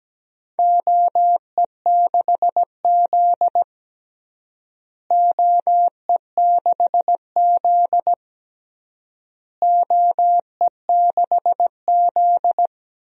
Morse code, heard as a keyed tone, about 17 wpm, 700 Hz. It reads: OE6Z OE6Z OE6Z
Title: DXCC-100 Call Signs - 4 Characters - Repeated 3x - 17wpm